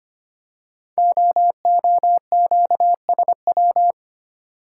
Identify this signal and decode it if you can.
Morse code, heard as a keyed tone, about 25 wpm, 700 Hz. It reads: OOQSW
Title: Code Groups - 5 Letters - 25wpm